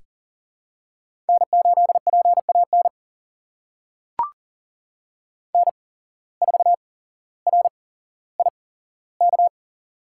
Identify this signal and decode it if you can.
Morse code, heard as a keyed tone, about 40 wpm, spaced as for 17 wpm, 700 Hz. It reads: N8PAN N4RIK
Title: USA Call Signs - 5 Characters - 8x Character Spacing - 40wpm